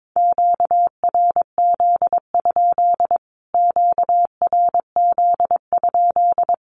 Morse code, heard as a keyed tone, about 22 wpm, 700 Hz. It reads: QRZ? QRZ?
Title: Q-Codes Repeated 3x 22wpm